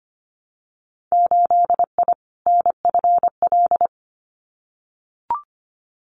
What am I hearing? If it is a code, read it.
Morse code, heard as a keyed tone, about 25 wpm, 700 Hz. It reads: 8I NFL
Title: Winter Field Day Exchanges Encore 25wpm